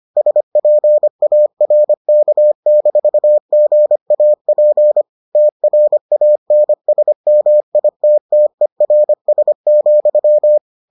Morse code, heard as a keyed tone, about 25 wpm, 600 Hz. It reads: SPARK-GAP TRANSMITTERS,